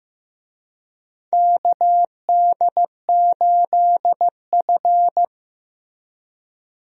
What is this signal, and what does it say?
Morse code, heard as a keyed tone, about 15 wpm, 700 Hz. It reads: KD8F
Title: USA Call Signs - 4 Characters - Repeated 2x - 15wpm